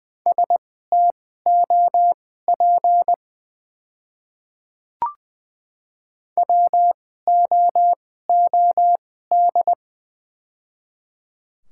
Morse code, heard as a keyed tone, about 20 wpm, spaced as for 14 wpm, 700 Hz. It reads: STOP WOOD
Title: 2x Character Spacing - Top 300 Words - 20wpm